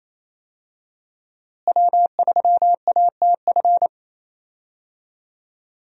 Morse code, heard as a keyed tone, about 28 wpm, 700 Hz. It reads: W3ATF
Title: USA Call Signs - 5 Characters - Repeated 2x - 28wpm